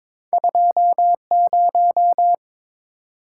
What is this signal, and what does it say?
Morse code, heard as a keyed tone, about 31 wpm, spaced as for 27 wpm, 700 Hz. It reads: 20
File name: Speed-Racing - Two Letter-Number Combinations - 22wpm